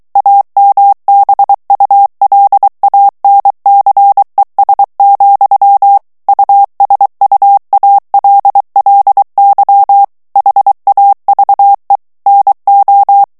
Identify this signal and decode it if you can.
Morse code, heard as a keyed tone, about 23 wpm, 800 Hz. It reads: AMBULANCES, USUALLY HAVE NO
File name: morse mp3 / ANGLAIS texte clair 2 vitesse 22 mots minute